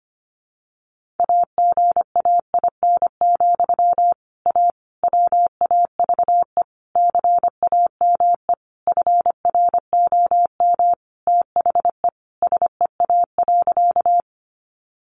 Morse code, heard as a keyed tone, about 25 wpm, 700 Hz. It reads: AGAIN, A WAVE CAME FROM THE SEA.
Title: Sentences from Top 1400 Words 25wpm